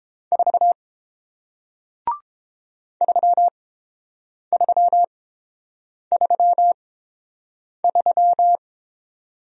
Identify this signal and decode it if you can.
Morse code, heard as a keyed tone, about 31 wpm, spaced as for 7 wpm, 700 Hz. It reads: 4 3333